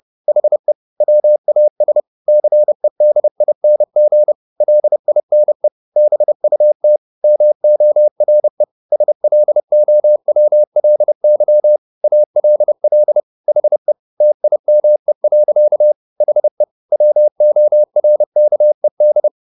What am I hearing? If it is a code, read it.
Morse code, heard as a keyed tone, about 30 wpm, 600 Hz. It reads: HE WAS CEDING LINE BUT MORE SLOWLY ALL HE TIME. HE WORKED